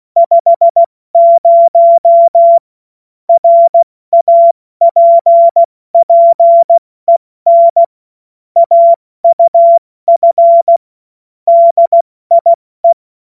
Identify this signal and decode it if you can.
Morse code, heard as a keyed tone, about 16 wpm, 680 Hz. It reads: 50 RAPPEN AUF DIE
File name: Morsekurs / Witze / Witz 15. November 2022 70 F80